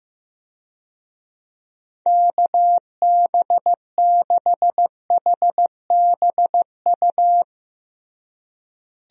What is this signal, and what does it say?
Morse code, heard as a keyed tone, about 15 wpm, 700 Hz. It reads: KB6HBU
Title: USA Call Signs - 6 Characters - Repeated 2x - 15wpm